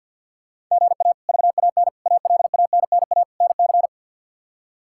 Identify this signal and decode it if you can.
Morse code, heard as a keyed tone, about 49 wpm, 700 Hz. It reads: GA VAN ALANNA NC
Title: K1USN SST QSO Practice - 50wpm